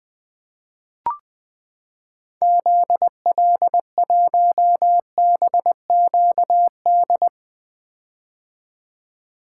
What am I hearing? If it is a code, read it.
Morse code, heard as a keyed tone, about 21 wpm, 700 Hz. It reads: ZL1BQD
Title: 2023 POTA Activator Call Signs - 20wpm